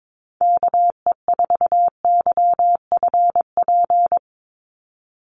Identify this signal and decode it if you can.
Morse code, heard as a keyed tone, about 22 wpm, 700 Hz. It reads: KE4YFP
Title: USA Call Signs 22wpm